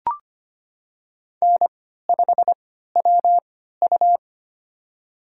Morse code, heard as a keyed tone, about 25 wpm, spaced as for 14 wpm, 700 Hz. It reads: N5WU
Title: USA Call Signs - 4 Characters - 3x Character Spacing - 25wpm